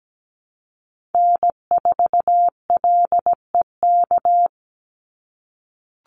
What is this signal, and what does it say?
Morse code, heard as a keyed tone, about 17 wpm, 700 Hz. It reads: N4LEK
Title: USA Call Signs 17wpm